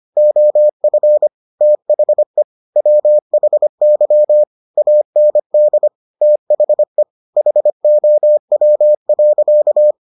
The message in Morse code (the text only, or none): OF THE WHY AND THE HOW.